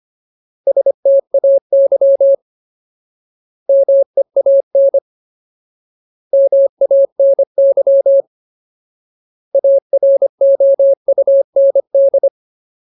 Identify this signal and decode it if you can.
Morse code, heard as a keyed tone, about 25 wpm, 550 Hz. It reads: STAY MEAN MANY AROUND
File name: Ultimate Word List / 25WPM Ultimate